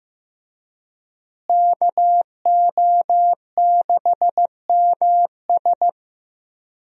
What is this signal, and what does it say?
Morse code, heard as a keyed tone, about 15 wpm, 700 Hz. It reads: KO6MS
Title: USA Call Signs - 5 Characters - Repeated 2x - 15wpm